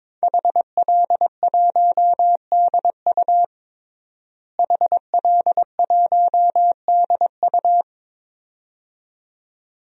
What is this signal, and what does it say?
Morse code, heard as a keyed tone, about 21 wpm, 700 Hz. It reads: HL1DU HL1DU
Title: DXCC-100 Call Signs - 5 Characters - Repeated 2x - 22wpm